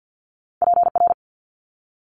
Morse code, heard as a keyed tone, about 50 wpm, 700 Hz. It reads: PR
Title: ARRL Sections 50wpm